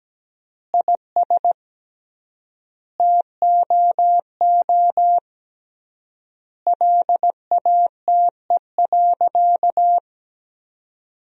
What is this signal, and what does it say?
Morse code, heard as a keyed tone, about 17 wpm, 700 Hz. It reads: IS TOO LATE.
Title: 3x Word-Spacing - Sentences from Top 300 Words - 17wpm